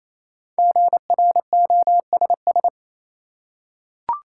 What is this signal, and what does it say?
Morse code, heard as a keyed tone, about 28 wpm, 700 Hz. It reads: GROSS
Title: Five Letter Words - 28wpm